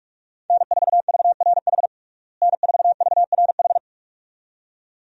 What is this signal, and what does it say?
Morse code, heard as a keyed tone, about 46 wpm, 700 Hz. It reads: N4VRH N4VRH
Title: USA Call Signs - 5 Characters - Repeated 2x - 45wpm